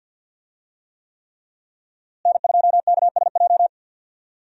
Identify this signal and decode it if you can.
Morse code, heard as a keyed tone, about 51 wpm, 700 Hz. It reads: N2KSJ
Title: USA Call Signs - 5 Characters - Repeated 3x - 50wpm